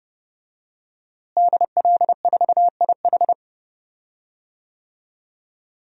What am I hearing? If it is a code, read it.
Morse code, heard as a keyed tone, about 32 wpm, 700 Hz. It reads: DL4IH
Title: DXCC-100 Call Signs - 5 Characters - Repeated 3x - 30wpm